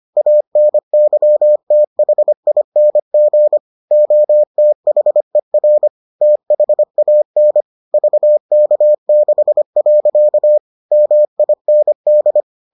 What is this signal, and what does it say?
Morse code, heard as a keyed tone, about 25 wpm, 600 Hz. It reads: ANYTHING OTHER THAN VK6. MIND